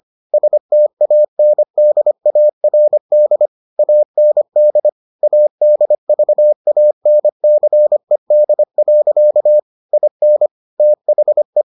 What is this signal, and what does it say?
Morse code, heard as a keyed tone, about 25 wpm, 600 Hz. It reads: STANDARD AND ADVANCED. IN THE